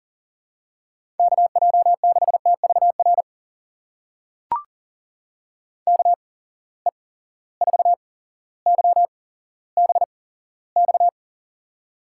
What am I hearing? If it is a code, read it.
Morse code, heard as a keyed tone, about 39 wpm, spaced as for 17 wpm, 700 Hz. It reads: KJ6TVR KE4YBX
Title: USA Call Signs - 6 Characters - 8x Character Spacing - 40wpm